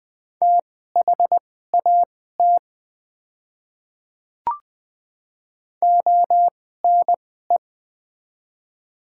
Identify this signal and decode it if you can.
Morse code, heard as a keyed tone, about 20 wpm, spaced as for 14 wpm, 700 Hz. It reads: THAT ONE
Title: 2x Character Spacing - Top 100 Words - 20wpm